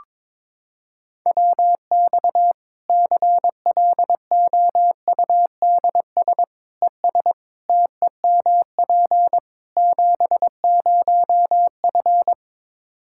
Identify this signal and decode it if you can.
Morse code, heard as a keyed tone, about 22 wpm, 700 Hz. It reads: WX CLOUDS ES TEMP 70F
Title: Practice QSO - Exchange 2 - Info - 22wpm